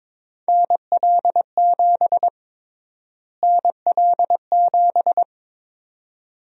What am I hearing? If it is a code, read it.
Morse code, heard as a keyed tone, about 22 wpm, 700 Hz. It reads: NL7 NL7
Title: USA Call Sign Prefixes - Repeated 2x - 22wpm